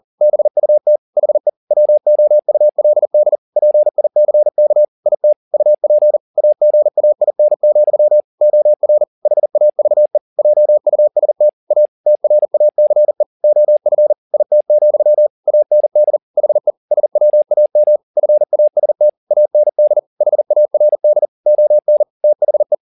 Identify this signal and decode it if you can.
Morse code, heard as a keyed tone, about 40 wpm, 600 Hz. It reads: BUT HE WOULD PICK IT UP AGAIN, OR HAVE JUST A TRACE OF IT, AND HE SWAM FAST AND HARD ON THE
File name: the old man and the sea 2 40wpm